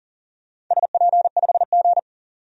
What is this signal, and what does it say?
Morse code, heard as a keyed tone, about 40 wpm, 700 Hz. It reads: SP5G
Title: DXCC-100 Call Signs - 4 Characters - 40wpm